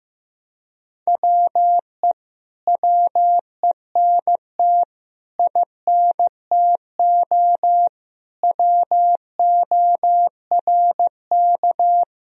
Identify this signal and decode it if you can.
Morse code, heard as a keyed tone, about 15 wpm, 700 Hz. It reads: WE WENT INTO WORK
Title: Sentences from Top 1000 Words - Review Part 4 - 15wpm